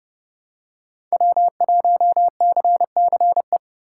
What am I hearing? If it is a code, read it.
Morse code, heard as a keyed tone, about 31 wpm, 700 Hz. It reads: W1CCE
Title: K1USN SST QSO Practice - 30wpm